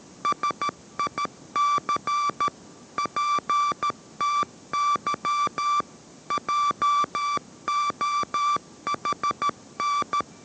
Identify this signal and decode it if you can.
Morse code, heard as a keyed tone, about 13 wpm, 1170 Hz. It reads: SIC PTY JOHN